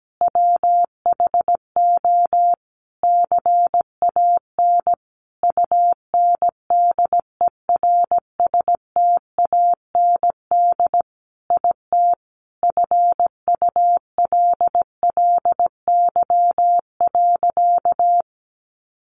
Morse code, heard as a keyed tone, about 17 wpm, 700 Hz. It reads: WHO CAN UNDERSTAND IT FULLY.
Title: Sentences from Top 1300 Words 17wpm